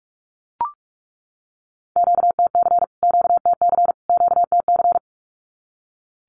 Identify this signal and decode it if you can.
Morse code, heard as a keyed tone, about 44 wpm, 700 Hz. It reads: QTC QTC QTC
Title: Q-Codes Repeated 3x 45wpm